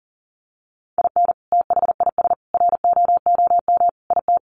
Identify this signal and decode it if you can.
Morse code, heard as a keyed tone, about 40 wpm, 700 Hz. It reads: IN THIS ROOM IT
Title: Sentences from Top 700 Words 40wpm